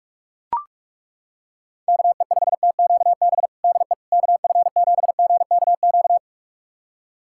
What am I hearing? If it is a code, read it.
Morse code, heard as a keyed tone, about 41 wpm, 700 Hz. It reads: KE5TQB DE KF7GKQ
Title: USA Callsign DE Callsign - 6 Characters - 45wpm